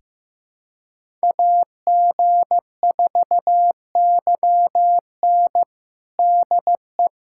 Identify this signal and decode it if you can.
Morse code, heard as a keyed tone, about 15 wpm, 700 Hz. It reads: AG4YN DE
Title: USA Callsign DE Callsign - 5 Characters - 15wpm